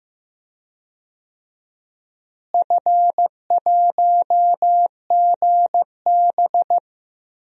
Morse code, heard as a keyed tone, about 15 wpm, 700 Hz. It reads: F1GB
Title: DXCC-100 Call Signs - 4 Characters - Repeated 2x - 15wpm